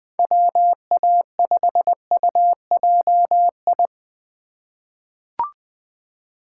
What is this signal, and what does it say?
Morse code, heard as a keyed tone, about 20 wpm, 700 Hz. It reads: WA5UJI
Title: USA Call Signs - 6 Characters - 3x Character Spacing - 20wpm